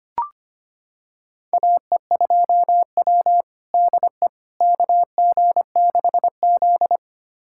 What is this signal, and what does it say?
Morse code, heard as a keyed tone, about 25 wpm, 700 Hz. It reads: AE2W DE KG6Z
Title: USA Callsign DE Callsign - 4 Characters - 25wpm